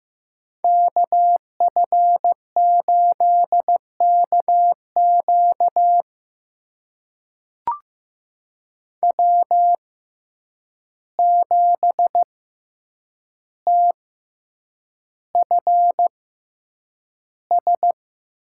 Morse code, heard as a keyed tone, about 15 wpm, spaced as for 8 wpm, 700 Hz. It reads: KF8KQ W7TFS
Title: USA Call Signs - 5 Characters - 6x Character Spacing - 15wpm